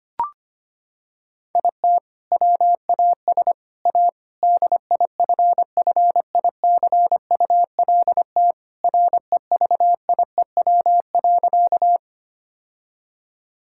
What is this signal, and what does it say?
Morse code, heard as a keyed tone, about 25 wpm, 700 Hz. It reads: IT WAS A DIFFICULT REVIEW.
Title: Sentences from Top 1000 Words - Review Part 6 - 25wpm